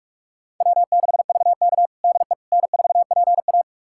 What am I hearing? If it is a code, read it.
Morse code, heard as a keyed tone, about 41 wpm, 690 Hz. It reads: W6VK DE N4PA
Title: USA Callsign DE Callsign - 4 Characters - 45wpm